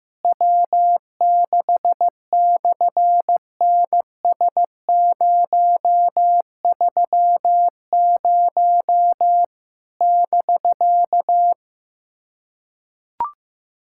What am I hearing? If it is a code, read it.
Morse code, heard as a keyed tone, about 14 wpm, 700 Hz. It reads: W6/NS030 <BK>
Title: SOTA QSO Practice 15wpm